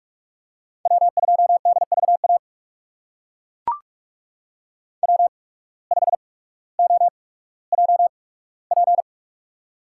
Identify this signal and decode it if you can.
Morse code, heard as a keyed tone, about 43 wpm, spaced as for 19 wpm, 700 Hz. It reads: W2DVA W5OJP
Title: USA Call Signs - 5 Characters - 8x Character Spacing - 45wpm